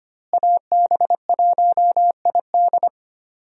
Morse code, heard as a keyed tone, about 25 wpm, 700 Hz. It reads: AB1ID
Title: USA Call Signs - 5 Characters - Repeated 3x - 25wpm